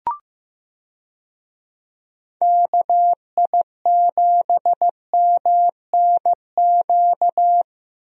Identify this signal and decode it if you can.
Morse code, heard as a keyed tone, about 15 wpm, 700 Hz. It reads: KI7MNQ
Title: USA Call Signs - 6 Characters - Repeated 2x - 15wpm